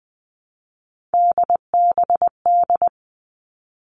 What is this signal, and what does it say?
Morse code, heard as a keyed tone, about 20 wpm, 700 Hz. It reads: DBD
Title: All Cut Numbers - 3-Digit 20wpm